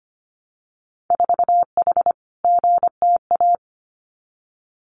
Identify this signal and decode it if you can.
Morse code, heard as a keyed tone, about 25 wpm, 700 Hz. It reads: 4H GTA
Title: Winter Field Day Exchanges Encore 25wpm